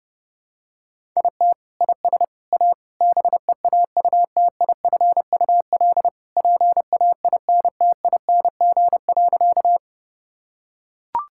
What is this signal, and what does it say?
Morse code, heard as a keyed tone, about 30 wpm, 700 Hz. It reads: IT IS A BEAUTIFUL PAINTING.
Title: Sentences from Top 1000 Words - Review Part 6 - 30wpm